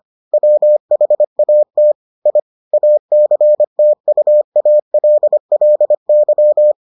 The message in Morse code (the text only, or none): WHAT I ACTUALLY